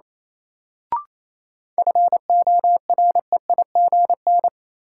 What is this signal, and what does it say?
Morse code, heard as a keyed tone, about 27 wpm, 700 Hz. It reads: FOREIGN
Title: Sentences from Top 1400 Words - New Word Drill - 28wpm